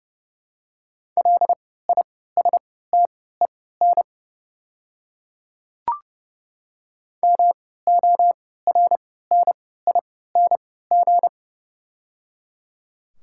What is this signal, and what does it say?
Morse code, heard as a keyed tone, about 30 wpm, spaced as for 17 wpm, 700 Hz. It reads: LISTEN MORNING
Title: 3x Character Spacing - Top 500 Words - 30wpm